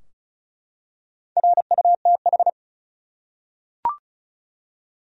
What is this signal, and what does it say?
Morse code, heard as a keyed tone, about 35 wpm, 700 Hz. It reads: RUTH
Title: USA Common Names - Max 4 Letters - 35wpm